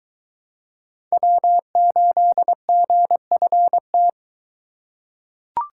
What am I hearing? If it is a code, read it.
Morse code, heard as a keyed tone, about 23 wpm, 700 Hz. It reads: W8GFT